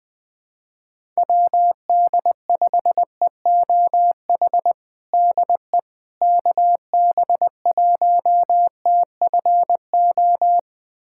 Morse code, heard as a keyed tone, about 21 wpm, 700 Hz. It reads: WD5EOH DE KB1TFO
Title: USA Callsign DE Callsign - 6 Characters - 20wpm